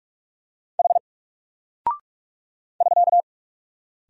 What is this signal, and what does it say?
Morse code, heard as a keyed tone, about 45 wpm, 700 Hz. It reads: H <SK>
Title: Single Character (Letter:Number:Punctuation:Prosign) 45wpm